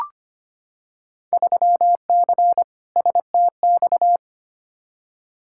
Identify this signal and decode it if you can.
Morse code, heard as a keyed tone, about 25 wpm, 700 Hz. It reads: 3C STX
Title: Field Day Exchanges Encore - 25wpm